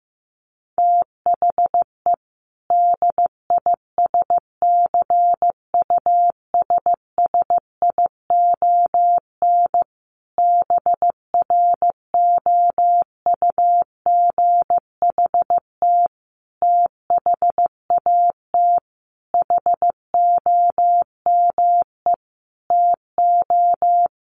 Morse code, heard as a keyed tone, about 15 wpm, 700 Hz. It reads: THE DISCUSSION BROUGHT THAT HOME TO
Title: Sentences from Top 1100 Words 15wpm